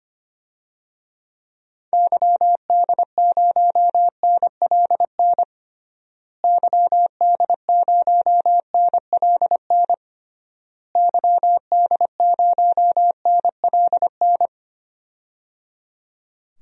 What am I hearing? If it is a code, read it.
Morse code, heard as a keyed tone, about 25 wpm, 700 Hz. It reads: YD0NLN YD0NLN YD0NLN
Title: DXCC-100 Call Signs - 6 Characters - Repeated 3x - 25wpm